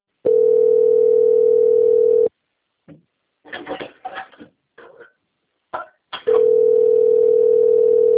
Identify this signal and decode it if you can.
Ringback tone (North America), 440 + 480 Hz.
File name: out- - - - - .20549